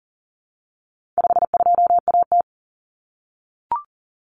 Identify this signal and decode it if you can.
Morse code, heard as a keyed tone, about 40 wpm, 700 Hz. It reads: 52AT
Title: Common Cut Numbers - 4-Digit 40wpm